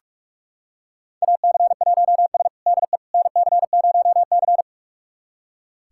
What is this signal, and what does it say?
Morse code, heard as a keyed tone, about 42 wpm, 700 Hz. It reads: AC1S DE NC0C
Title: USA Callsign DE Callsign - 4 Characters - 45wpm